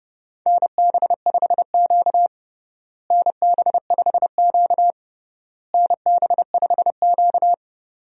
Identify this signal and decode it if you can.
Morse code, heard as a keyed tone, about 32 wpm, 700 Hz. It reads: NB5Q NB5Q NB5Q
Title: USA Call Signs - 4 Characters - Repeated 3x - 30wpm